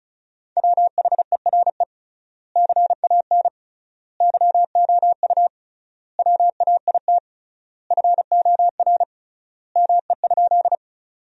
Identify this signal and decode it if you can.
Morse code, heard as a keyed tone, about 34 wpm, 700 Hz. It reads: WHERE CAN YOU WAIT FOR ME?
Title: 3x Word-Spacing - Sentences from Top 500 Words - 35wpm